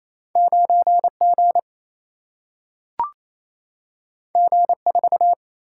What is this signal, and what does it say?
Morse code, heard as a keyed tone, about 28 wpm, 700 Hz. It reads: 9 G G 4
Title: Single-Letter Single-Number 28wpm